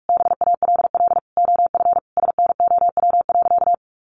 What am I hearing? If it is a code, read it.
Morse code, heard as a keyed tone, about 44 wpm, 700 Hz. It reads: BALL OF SNOW.
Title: Easy Phrases 45wpm